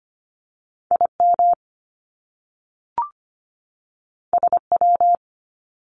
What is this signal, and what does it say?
Morse code, heard as a keyed tone, about 25 wpm, 700 Hz. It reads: IM SW
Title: Top Two Letter Combinations in Words 25wpm